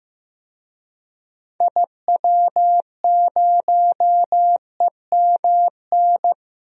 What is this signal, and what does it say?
Morse code, heard as a keyed tone, about 15 wpm, 700 Hz. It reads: IW0EMN
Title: DXCC-100 Call Signs - 6 Characters - 15wpm